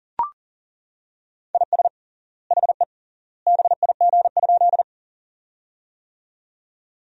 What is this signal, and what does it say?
Morse code, heard as a keyed tone, about 39 wpm, 700 Hz. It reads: IS HE BIG?
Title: 3x Word-Spacing - Sentences from Top 200 Words - 40wpm